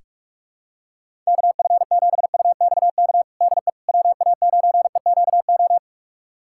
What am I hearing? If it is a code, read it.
Morse code, heard as a keyed tone, about 43 wpm, 700 Hz. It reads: KF7UXK DE WA9EQO
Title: USA Callsign DE Callsign 45wpm